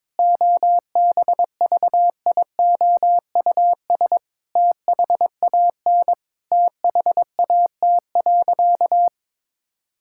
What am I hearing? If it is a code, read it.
Morse code, heard as a keyed tone, about 22 wpm, 700 Hz. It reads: OBVIOUS THAN THAT.
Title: Sentences from Top 1600 Words 22wpm